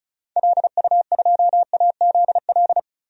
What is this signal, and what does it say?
Morse code, heard as a keyed tone, about 36 wpm, 690 Hz. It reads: LU2AZL
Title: DXCC-100 Call Signs - 6 Characters - 35wpm